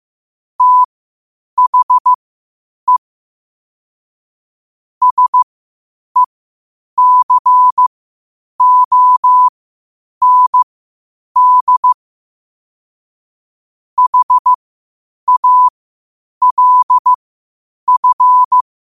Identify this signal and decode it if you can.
Morse code, heard as a keyed tone, about 15 wpm, spaced as for 8 wpm, 1000 Hz. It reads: THE SECOND HALF
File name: qotdfiles / 8wpm